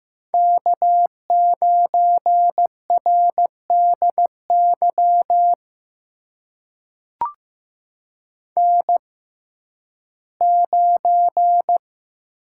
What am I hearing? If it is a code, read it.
Morse code, heard as a keyed tone, about 15 wpm, spaced as for 11 wpm, 700 Hz. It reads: K9RDY N9
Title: USA Call Signs - 6x Character Spacing - 15wpm